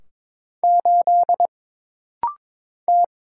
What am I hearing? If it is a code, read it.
Morse code, heard as a keyed tone, about 22 wpm, 700 Hz. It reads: 8 T
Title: Single Letter-Number - Rapid-Fire 22wpm